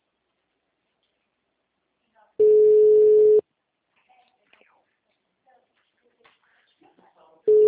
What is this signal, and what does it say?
Ringback tone (Europe, 425 Hz).
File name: out- -2003- - - .22921